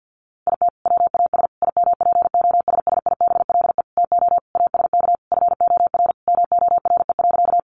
Morse code, heard as a keyed tone, about 50 wpm, 700 Hz. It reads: IT WAS IMPOSSIBLE TO ASK FOR MORE.
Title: Sentences from Top 1100 Words 50wpm